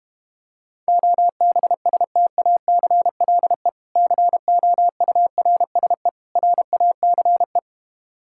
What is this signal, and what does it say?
Morse code, heard as a keyed tone, about 32 wpm, 690 Hz. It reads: OBSTACLE COURSE RACE